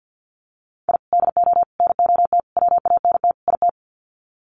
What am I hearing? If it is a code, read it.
Morse code, heard as a keyed tone, about 50 wpm, 700 Hz. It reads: I DO NOT WANT IT
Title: Sets of 5 Words Encore2 50wpm